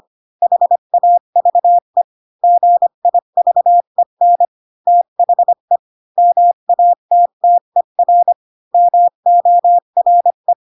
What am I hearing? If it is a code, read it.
Morse code, heard as a keyed tone, about 25 wpm, 700 Hz. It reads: HAVE GIVEN THE MATTER MORE
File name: Pinocchio / Pinocchio ch3 20 25 ik1hge